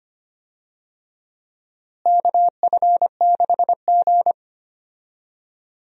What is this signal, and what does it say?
Morse code, heard as a keyed tone, about 25 wpm, 700 Hz. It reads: KF6G
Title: USA Call Signs - 4 Characters - Repeated 3x - 25wpm